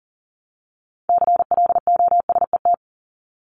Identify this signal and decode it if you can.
Morse code, heard as a keyed tone, about 40 wpm, 700 Hz. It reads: CLOSET
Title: Top 1751-2000 Words 40wpm